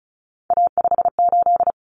Morse code, heard as a keyed tone, about 34 wpm, 700 Hz. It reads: A58
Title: Common Cut Numbers - 3-Digit 35wpm